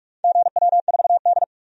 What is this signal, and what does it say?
Morse code, heard as a keyed tone, about 41 wpm, 700 Hz. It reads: GW4D
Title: DXCC-100 Call Signs - 4 Characters - 45wpm